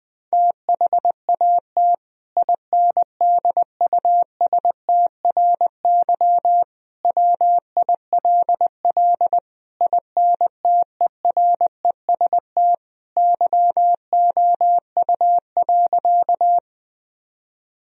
Morse code, heard as a keyed tone, about 20 wpm, 700 Hz. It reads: THAT INDUSTRY WILL INTEREST YOU.
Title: Sentences from Top 1000 Words - Review Part 4 - 20wpm